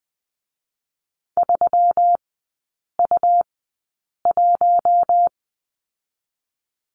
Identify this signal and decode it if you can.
Morse code, heard as a keyed tone, about 20 wpm, 700 Hz. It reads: 3 U 1
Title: 133 Lesson 22e - Question Mark - Three Characters 20wpm